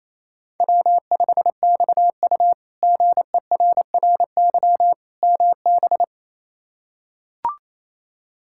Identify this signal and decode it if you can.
Morse code, heard as a keyed tone, about 28 wpm, 700 Hz. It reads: W5XU GERRY MB
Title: K1USN SST QSO Practice - 28wpm